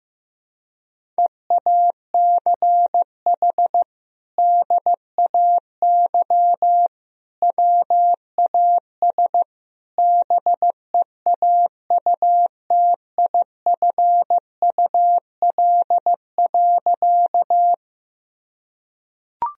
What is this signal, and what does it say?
Morse code, heard as a keyed tone, about 15 wpm, 700 Hz. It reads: EACH DAY WAS BEAUTIFUL.
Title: Sentences from Top 1000 Words - Review Part 1 - 15wpm